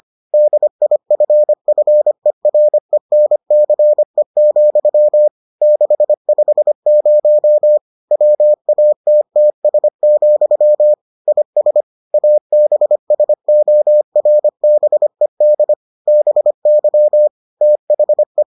Morse code, heard as a keyed tone, about 25 wpm, 600 Hz. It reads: DIFFERENCE, 650 WATTS, IS ABSORBED BY THE